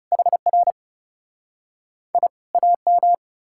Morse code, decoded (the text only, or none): HR I AM